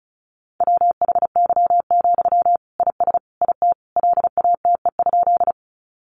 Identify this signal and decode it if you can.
Morse code, heard as a keyed tone, about 34 wpm, 700 Hz. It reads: WHY, IS IT LATE?
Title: Sentences from Top 300 Words 35wpm